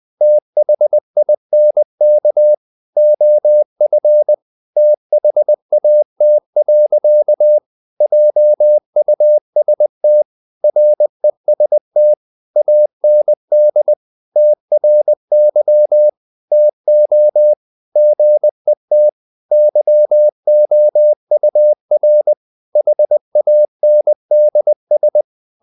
Morse code, heard as a keyed tone, about 20 wpm, 600 Hz. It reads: THINK OF THAT. JUST REST AND TRY TO GET YOUR HANDS
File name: the old man and the sea part 2 20wpm